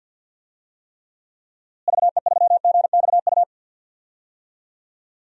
Morse code, heard as a keyed tone, about 47 wpm, 700 Hz. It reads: VE3GXU
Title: DXCC-100 Call Signs - 6 Characters - Repeated 2x - 50wpm